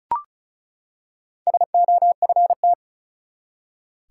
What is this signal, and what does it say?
Morse code, heard as a keyed tone, about 36 wpm, 700 Hz. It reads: SOFT
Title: Four Letter Words - 35wpm